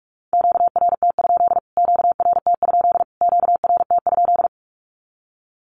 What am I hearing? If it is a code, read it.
Morse code, heard as a keyed tone, about 44 wpm, 700 Hz. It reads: QRT? QRT? QRT?
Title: Q-Codes Repeated 3x 45wpm